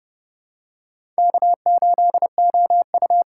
Morse code, heard as a keyed tone, about 29 wpm, 700 Hz. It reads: K8OU
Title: USA Call Signs - 4 Characters - 3x Character Spacing - 30wpm